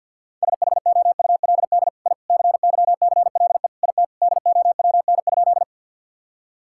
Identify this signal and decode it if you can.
Morse code, heard as a keyed tone, about 51 wpm, 700 Hz. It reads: SHOULD I CYCLE IT DOWN?
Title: Sentences from Top 1000 Words - Review Part 6 - 50wpm